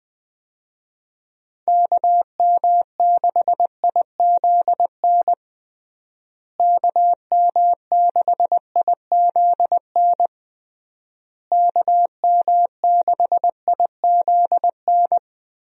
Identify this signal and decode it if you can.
Morse code, heard as a keyed tone, about 20 wpm, 700 Hz. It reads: KM6IZN KM6IZN KM6IZN
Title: USA Call Signs - 6 Characters - Repeated 3x - 20wpm